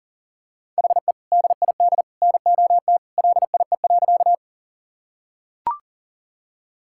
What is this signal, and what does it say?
Morse code, heard as a keyed tone, about 40 wpm, 700 Hz. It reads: HE DID NOT LIE.
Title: Sentences from Top 1600 Words 40wpm